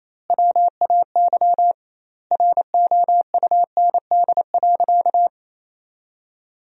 Morse code, heard as a keyed tone, about 28 wpm, 700 Hz. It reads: WAY ROUND.
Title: 2x Word-Spacing - Sentences from Top 200 Words - 28wpm